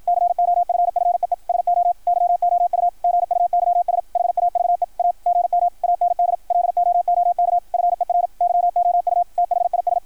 Morse code, heard as a keyed tone, about 53 wpm, 690 Hz, with radio noise. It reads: COFFEE SO YOU GUYS HAVE A GM AND LOOK FER YOU THIS
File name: archived / CRSnet